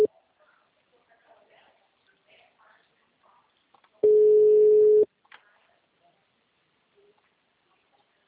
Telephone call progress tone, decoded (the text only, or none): ringback tone